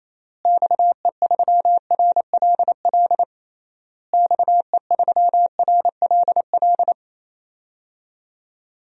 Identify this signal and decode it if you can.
Morse code, heard as a keyed tone, about 28 wpm, 700 Hz. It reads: XE3RLL XE3RLL
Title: DXCC-100 Call Signs - 6 Characters - Repeated 3x - 28wpm